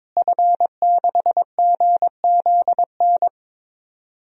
Morse code, heard as a keyed tone, about 22 wpm, 700 Hz. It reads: F6GZN
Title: DXCC-100 Call Signs - 5 Characters - Repeated 3x - 22wpm